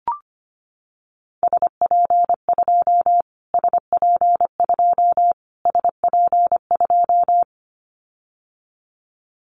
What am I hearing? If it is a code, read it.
Morse code, heard as a keyed tone, about 25 wpm, 700 Hz. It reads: SP2 SP2 SP2
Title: Callsign Prefixes for Top 25 DXCC - Repeated 3x - 25wpm